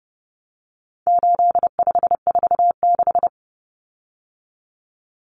Morse code, heard as a keyed tone, about 30 wpm, 700 Hz. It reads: 8546
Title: Common Cut Numbers - 4-Digit 30wpm